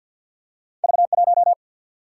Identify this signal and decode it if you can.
Morse code, heard as a keyed tone, about 47 wpm, 700 Hz. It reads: V1
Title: Single-Letter Single-Number 50wpm